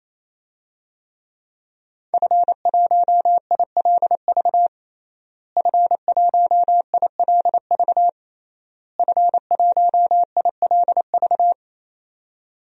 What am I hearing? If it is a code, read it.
Morse code, heard as a keyed tone, about 28 wpm, 700 Hz. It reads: F1ILV F1ILV F1ILV
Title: DXCC-100 Call Signs - 5 Characters - Repeated 3x - 28wpm